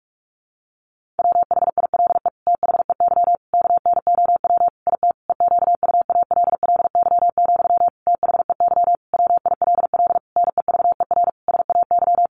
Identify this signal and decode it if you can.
Morse code, heard as a keyed tone, about 44 wpm, 700 Hz. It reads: WHILE THEY KNOW IT EQUALLY, THEY WILL NEVER SAY
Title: Sentences from Top 1400 Words 45wpm